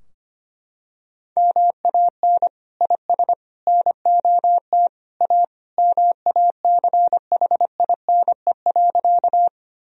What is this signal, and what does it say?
Morse code, heard as a keyed tone, about 25 wpm, 700 Hz. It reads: MAN IS NOT A MACHINE.
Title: Sentences from Top 1000 Words - Review Part 2 - 25wpm